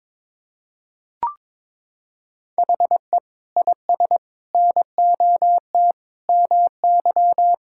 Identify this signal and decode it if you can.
Morse code, heard as a keyed tone, about 22 wpm, 700 Hz. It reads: HE IS NOT MY
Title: Sentences from Top 1600 Words 22wpm